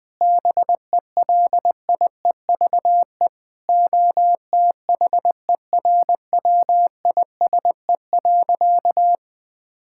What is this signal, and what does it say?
Morse code, heard as a keyed tone, about 20 wpm, 700 Hz. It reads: BELIEVE OTHERWISE.
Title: Sentences with 6-Word Phrases - 20wpm